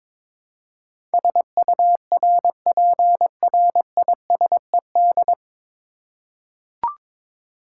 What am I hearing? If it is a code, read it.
Morse code, heard as a keyed tone, about 22 wpm, 700 Hz. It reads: SURPRISED
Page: Morse Code Ninja practice 22wpm